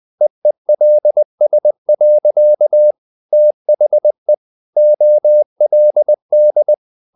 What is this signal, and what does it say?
Morse code, heard as a keyed tone, about 20 wpm, 600 Hz. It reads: EELS. THE OLD
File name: the old man and the sea part 2 20wpm